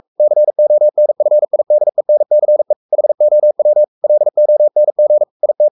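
Morse code, heard as a keyed tone, about 44 wpm, 600 Hz. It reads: CONFIDENCE HOW LONG IT
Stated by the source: Morse code at 43 wpm sending Ken Follett's novel Never in English, from CW med Gnister.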